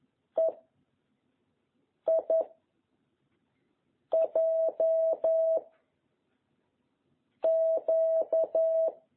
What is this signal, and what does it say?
Morse code, heard as a keyed tone, about 11 wpm, 660 Hz. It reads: E I J Q
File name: jh8cbh / 2021cw